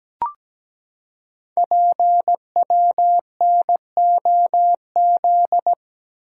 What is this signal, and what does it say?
Morse code, heard as a keyed tone, about 17 wpm, 700 Hz. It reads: PWNOZ
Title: Code Groups - 5 Letters - 17wpm